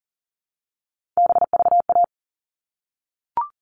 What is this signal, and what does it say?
Morse code, heard as a keyed tone, about 40 wpm, 700 Hz. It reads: BVA
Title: All Cut Numbers - 3-Digit 40wpm